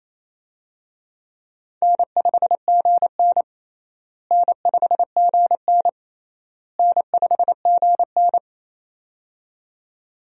Morse code, heard as a keyed tone, about 28 wpm, 700 Hz. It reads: N5GN N5GN N5GN
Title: USA Call Signs - 4 Characters - Repeated 3x - 28wpm